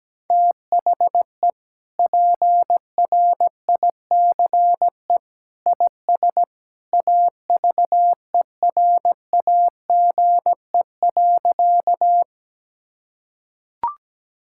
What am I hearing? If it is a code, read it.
Morse code, heard as a keyed tone, about 17 wpm, 700 Hz. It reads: THE PRICE IS AVERAGE.
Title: Sentences from Top 1000 Words - Review Part 6 - 17wpm